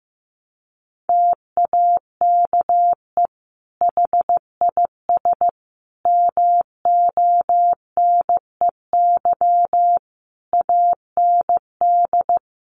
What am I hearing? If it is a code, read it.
Morse code, heard as a keyed tone, about 14 wpm, 700 Hz. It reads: TAKE HIS MONEY AND
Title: Sentences from Top 1300 Words 15wpm